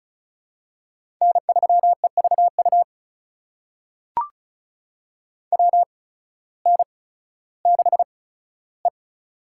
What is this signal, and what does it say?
Morse code, heard as a keyed tone, about 36 wpm, spaced as for 16 wpm, 700 Hz. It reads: N3EVU WN6E
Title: USA Call Signs - 8x Character Spacing - 35wpm